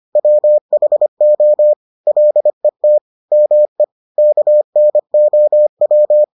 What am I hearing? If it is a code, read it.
Morse code, heard as a keyed tone, about 25 wpm, 600 Hz. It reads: WHO LET ME KNOW